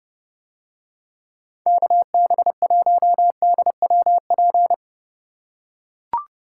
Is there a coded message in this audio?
Yes: Morse code from 1.7 to 4.7 s, heard as a keyed tone, about 29 wpm, 700 Hz. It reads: KB1DWP